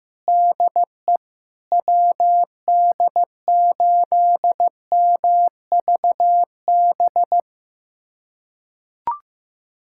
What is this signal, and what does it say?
Morse code, heard as a keyed tone, about 15 wpm, 700 Hz. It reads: DE WD8MVB
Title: USA Callsign DE Callsign 15wpm